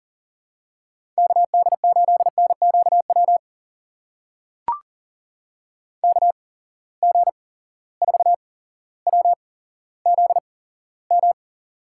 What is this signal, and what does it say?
Morse code, heard as a keyed tone, about 38 wpm, spaced as for 17 wpm, 700 Hz. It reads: KD8NQW KG4WZM
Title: USA Call Signs - 6 Characters - 8x Character Spacing - 40wpm